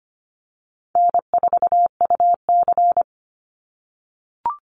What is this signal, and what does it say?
Morse code, heard as a keyed tone, about 25 wpm, 700 Hz. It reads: N4UC
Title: Call Signs - 25wpm